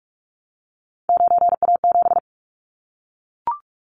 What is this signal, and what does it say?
Morse code, heard as a keyed tone, about 44 wpm, 700 Hz. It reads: 9A7
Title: Callsign Prefixes for Top 50 DXCC - 45wpm